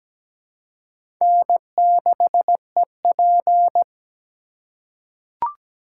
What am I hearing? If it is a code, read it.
Morse code, heard as a keyed tone, about 17 wpm, 700 Hz. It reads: N6EP
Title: USA Call Signs - 3x Character Spacing - 17wpm